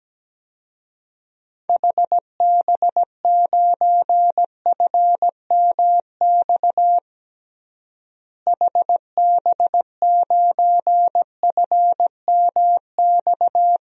Morse code, heard as a keyed tone, about 17 wpm, 700 Hz. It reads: HB9FMX HB9FMX
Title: DXCC-100 Call Signs - 6 Characters - Repeated 2x - 17wpm